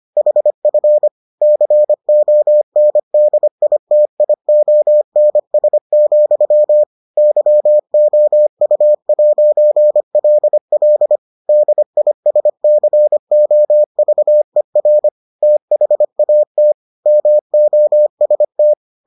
Morse code, heard as a keyed tone, about 25 wpm, 600 Hz. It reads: HF CONDITIONS, YOU'LL DISCOVER THAT MOST